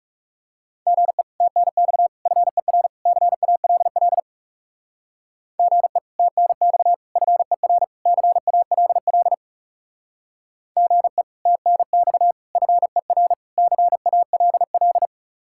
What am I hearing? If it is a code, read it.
Morse code, heard as a keyed tone, about 38 wpm, 700 Hz. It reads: GE TNX FER CALL GE TNX FER CALL GE TNX FER CALL